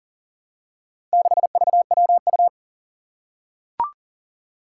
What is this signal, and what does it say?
Morse code, heard as a keyed tone, about 40 wpm, 700 Hz. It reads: 6VWU
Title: Four Challenging Letter-Number Combinations 40wpm